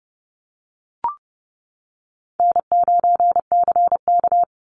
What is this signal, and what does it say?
Morse code, heard as a keyed tone, about 30 wpm, 700 Hz. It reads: N9CK
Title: Easy Call Signs - 30wpm